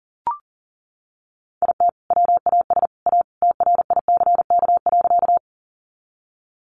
Morse code, heard as a keyed tone, about 40 wpm, 700 Hz. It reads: IT WAS A TRICK.
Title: Sentences from Top 1500 Words 40wpm